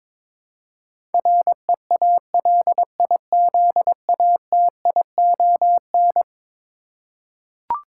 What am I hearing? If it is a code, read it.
Morse code, heard as a keyed tone, about 22 wpm, 700 Hz. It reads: REALIZATION
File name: practice / 22wpm / Eleven Letter Words - 22wpm